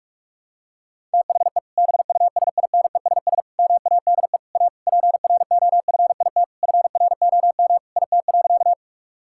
Morse code, heard as a keyed tone, about 46 wpm, 700 Hz. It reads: THE BUSINESS MADE A PROFIT FROM IT.
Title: Sentences from Top 1000 Words - Review Part 5 - 45wpm